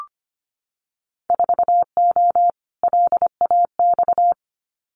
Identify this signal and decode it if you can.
Morse code, heard as a keyed tone, about 25 wpm, 700 Hz. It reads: 4O LAX
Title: Winter Field Day Exchanges Encore 25wpm